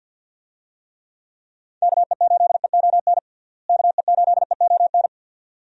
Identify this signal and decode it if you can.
Morse code, heard as a keyed tone, about 46 wpm, 700 Hz. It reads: KE8EON KE8EON
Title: USA Call Signs - 6 Characters - Repeated 2x - 50wpm